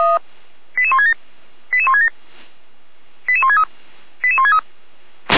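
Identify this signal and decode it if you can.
Five-tone selective call (ZVEI), heard as quick runs of five tones, tones of about 70 ms. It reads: 88167 88167 88163 88163
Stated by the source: ZVEI five-tone selective call received on an SDR.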